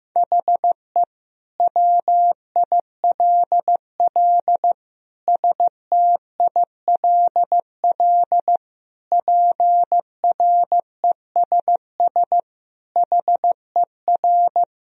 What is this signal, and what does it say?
Morse code, heard as a keyed tone, about 15 wpm, 700 Hz. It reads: HE WILL STILL PRESS HER
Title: Sentences from Top 1000 Words - Review Part 1 - 15wpm